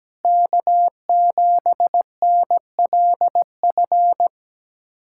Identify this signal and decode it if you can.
Morse code, heard as a keyed tone, about 17 wpm, 700 Hz. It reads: K7NLF
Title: USA Callsign DE Callsign 17wpm